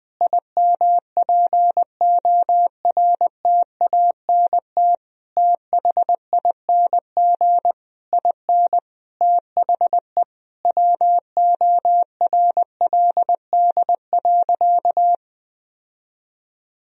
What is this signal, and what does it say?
Morse code, heard as a keyed tone, about 20 wpm, 700 Hz. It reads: IMPORTANT THING IN THE WORLD.
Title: Sentences from Top 1000 Words - Review Part 1 - 20wpm